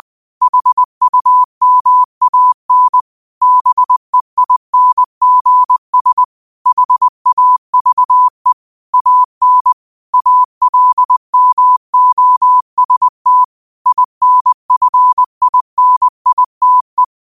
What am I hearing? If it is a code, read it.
Morse code, heard as a keyed tone, about 20 wpm, 1000 Hz. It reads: HUMAN BEINGS HAVE AN ALMOST INFINITE
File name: qotdfiles / 20wpm